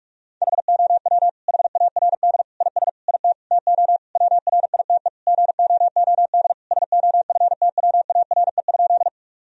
Morse code, heard as a keyed tone, about 46 wpm, 700 Hz. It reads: HOW HARD IS IT TO WRITE GOOD SOFTWARE?
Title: Sentences from Top 1000 Words - Review Part 3 - 45wpm